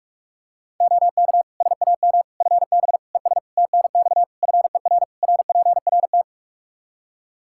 Morse code, heard as a keyed tone, about 42 wpm, 700 Hz. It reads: OK SAM FB ES TNX FER RPRT
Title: Practice QSO - Exchange 2 - Info - 45wpm